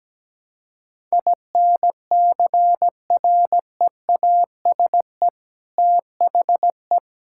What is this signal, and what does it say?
Morse code, heard as a keyed tone, about 17 wpm, 700 Hz. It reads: INCREASE THE
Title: Sentences from Top 1000 Words - Review Part 6 - 17wpm